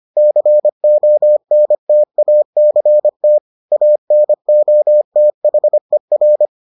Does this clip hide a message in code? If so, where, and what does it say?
Yes: Morse code from 0.2 to 6.5 s, heard as a keyed tone, about 25 wpm, 600 Hz. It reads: CONTACT ANOTHER